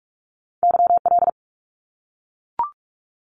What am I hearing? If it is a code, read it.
Morse code, heard as a keyed tone, about 44 wpm, 700 Hz. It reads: YL
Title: Abbreviations 45wpm